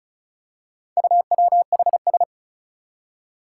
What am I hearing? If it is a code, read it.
Morse code, heard as a keyed tone, about 36 wpm, 700 Hz. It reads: UWHS
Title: Four Challenging Letter-Number Combinations 35wpm